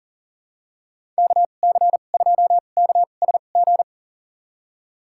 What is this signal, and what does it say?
Morse code, heard as a keyed tone, about 40 wpm, spaced as for 29 wpm, 700 Hz. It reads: KC2KSG
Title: USA Call Signs - 2x Character Spacing - 40wpm